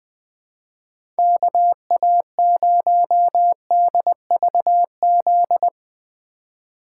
Morse code, heard as a keyed tone, about 20 wpm, 700 Hz. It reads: KA0DVZ
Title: USA Call Signs - 6 Characters - 3x Character Spacing - 20wpm